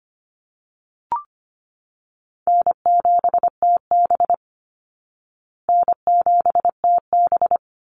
Morse code, heard as a keyed tone, about 25 wpm, 700 Hz. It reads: N7TB N7TB
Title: Call Signs Repeated 3x - 25wpm